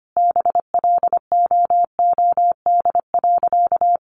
Morse code, heard as a keyed tone, about 25 wpm, 700 Hz. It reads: BLOOD.
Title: Sentences from Top 1200 Words 25wpm